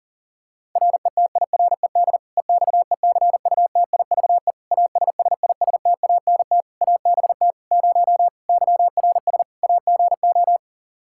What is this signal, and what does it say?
Morse code, heard as a keyed tone, about 40 wpm, 700 Hz. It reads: RETIRED EXECUTIVE ASSISTANT ABT 0 YRS AGO
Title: K1USN SST QSO Practice 40wpm